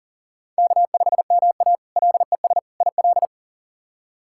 Morse code, heard as a keyed tone, about 40 wpm, 700 Hz. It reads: K5MA LES IL
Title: K1USN SST QSO Practice - 40wpm